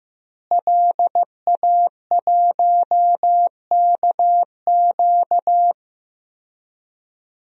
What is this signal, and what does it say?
Morse code, heard as a keyed tone, about 15 wpm, 700 Hz. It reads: LA1KQ
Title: DXCC-100 Call Signs - 5 Characters - Repeated 3x - 15wpm